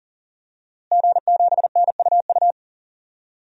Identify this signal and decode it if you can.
Morse code, heard as a keyed tone, about 40 wpm, 690 Hz. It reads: G7NUU
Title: DXCC-100 Call Signs - 5 Characters - Repeated 2x - 40wpm